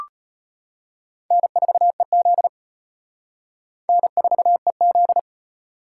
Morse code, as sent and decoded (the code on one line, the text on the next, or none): -. ....- . --.. / -. ....- . --..
N4EZ N4EZ